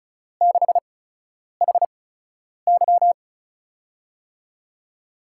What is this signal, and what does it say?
Morse code, heard as a keyed tone, about 36 wpm, spaced as for 10 wpm, 700 Hz. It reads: 6HY
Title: USA Call Signs - 8x Character Spacing - 35wpm